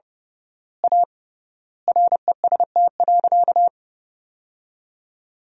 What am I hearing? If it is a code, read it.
Morse code, heard as a keyed tone, about 30 wpm, 700 Hz. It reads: A REST.
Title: 3x Word-Spacing - Sentences from Top 500 Words - 30wpm